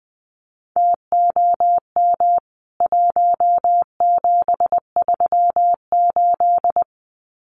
Morse code, heard as a keyed tone, about 20 wpm, 700 Hz. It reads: TOM 1738
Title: CWT Exchange 20wpm